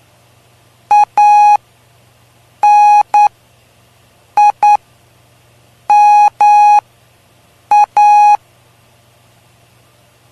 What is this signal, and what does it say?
Morse code, heard as a keyed tone, about 9 wpm, spaced as for 6 wpm, 830 Hz. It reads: ANIMA
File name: cw / py2lcd / 03-PY2LCD-Li��o 3